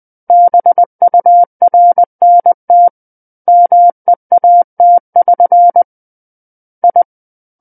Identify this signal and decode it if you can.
Morse code, heard as a keyed tone, about 20 wpm, 700 Hz. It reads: BURNT MEAT<SN> I
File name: 20 WPM